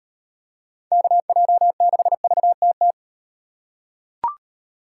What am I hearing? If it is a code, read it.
Morse code, heard as a keyed tone, about 38 wpm, 700 Hz. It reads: KJ6VTT